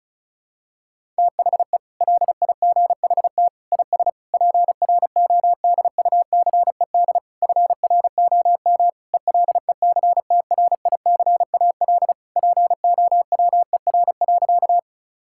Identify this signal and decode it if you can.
Morse code, heard as a keyed tone, about 36 wpm, 700 Hz. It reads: THE LIGHT IS PRODUCED FROM ELECTRICAL POWER.
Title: Sentences from Top 1000 Words - Review Part 6 - 35wpm